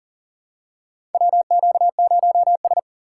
Morse code, heard as a keyed tone, about 40 wpm, 700 Hz. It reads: WQ0S
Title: USA Call Signs - 4 Characters - 3x Character Spacing - 40wpm